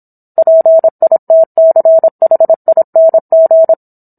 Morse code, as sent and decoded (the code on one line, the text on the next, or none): .--. .. - -.-. .... .. -. --.
PITCHING